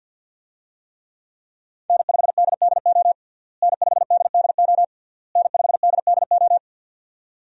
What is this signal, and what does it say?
Morse code, heard as a keyed tone, about 46 wpm, 700 Hz. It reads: N5DDO N5DDO N5DDO
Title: USA Call Signs - 5 Characters - Repeated 3x - 50wpm